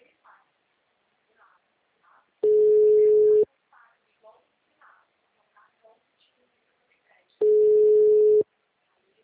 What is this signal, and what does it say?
Ringback tone (Europe, 425 Hz), with speech.